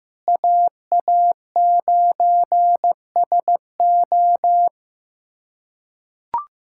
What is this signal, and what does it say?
Morse code, heard as a keyed tone, about 15 wpm, 700 Hz. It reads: AA9SO
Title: USA Callsign DE Callsign 15wpm